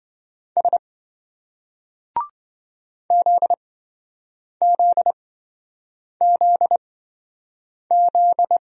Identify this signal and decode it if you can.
Morse code, heard as a keyed tone, about 26 wpm, spaced as for 7 wpm, 700 Hz. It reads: S ZZZZ